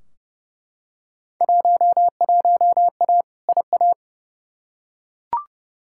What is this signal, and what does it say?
Morse code, heard as a keyed tone, about 31 wpm, 700 Hz. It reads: 11A IA
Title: Field Day Exchanges Encore - 30wpm